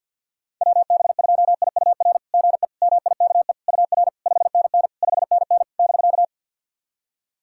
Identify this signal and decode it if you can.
Morse code, heard as a keyed tone, about 52 wpm, 690 Hz. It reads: WB2IUR GE MIKE UR 5NN 5NN <BK>
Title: SOTA QSO Practice 50wpm